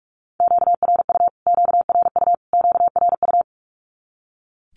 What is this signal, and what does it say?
Morse code, heard as a keyed tone, about 44 wpm, 700 Hz. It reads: QRU QRU QRU
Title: Q-Codes Repeated 3x 45wpm